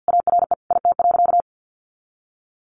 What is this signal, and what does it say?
Morse code, heard as a keyed tone, about 50 wpm, 700 Hz. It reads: ARE IT.
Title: Additional Easy Phrases 50wpm